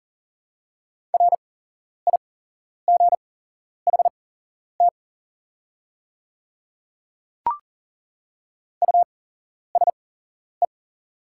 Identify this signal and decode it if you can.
Morse code, heard as a keyed tone, about 38 wpm, spaced as for 11 wpm, 690 Hz. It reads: RIGHT USE
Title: 8x Character Spacing - Top 300 Words - 40wpm